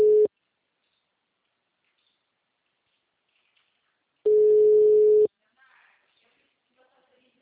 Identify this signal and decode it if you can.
Ringback tone (Europe, 425 Hz).
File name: out- -2010- - - .54883